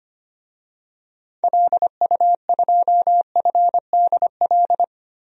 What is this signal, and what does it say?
Morse code, heard as a keyed tone, about 25 wpm, 700 Hz. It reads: LU2FDL
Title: DXCC-100 Call Signs - 6 Characters - Repeated 3x - 25wpm